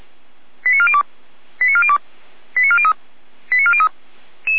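Five-tone selective call (ZVEI), heard as quick runs of five tones, tones of about 80 ms. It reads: 89402 89402 89403 89403
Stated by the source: ZVEI five-tone call 88190 received by SDR.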